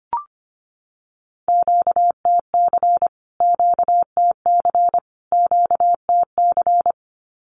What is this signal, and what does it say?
Morse code, heard as a keyed tone, about 25 wpm, 700 Hz. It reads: QTC QTC QTC
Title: Q-Codes Repeated 3x 25wpm